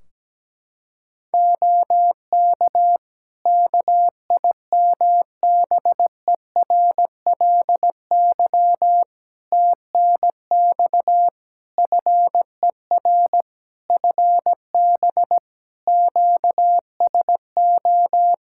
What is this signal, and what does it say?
Morse code, heard as a keyed tone, about 17 wpm, 700 Hz. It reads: OK KIMBERLY TNX FER FB QSO
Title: Practice QSO - Exchange 4 - Ending - 17wpm